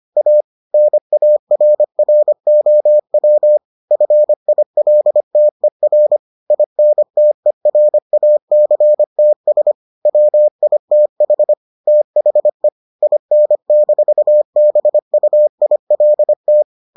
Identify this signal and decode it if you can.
Morse code, heard as a keyed tone, about 25 wpm, 600 Hz. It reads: A NARROW FILTER INTERACTS WITH THE IN-BUILT